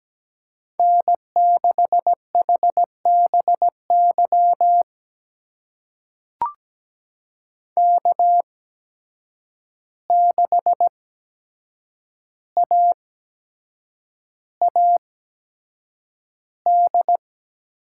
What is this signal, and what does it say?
Morse code, heard as a keyed tone, about 17 wpm, spaced as for 7 wpm, 700 Hz. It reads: N6HBY K6AAD
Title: USA Call Signs - 5 Characters - 8x Character Spacing - 17wpm